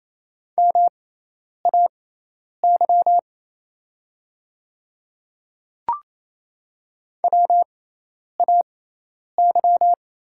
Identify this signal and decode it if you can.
Morse code, heard as a keyed tone, about 28 wpm, spaced as for 10 wpm, 700 Hz. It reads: MAY WAY
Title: 6x Character Spacing - Top 100 Words - 28wpm